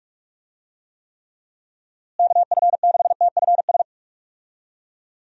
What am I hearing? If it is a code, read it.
Morse code, heard as a keyed tone, about 46 wpm, 700 Hz. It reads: KF6TFS
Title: USA Call Signs - 6 Characters - Repeated 2x - 45wpm